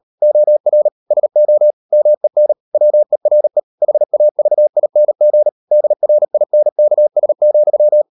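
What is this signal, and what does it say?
Morse code, heard as a keyed tone, about 38 wpm, 600 Hz. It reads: OR SO MEN WERE HAVING DRINKS,
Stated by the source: Morse code practice at 38 WPM, sending Ken Follett's novel Never in English, from CW med Gnister.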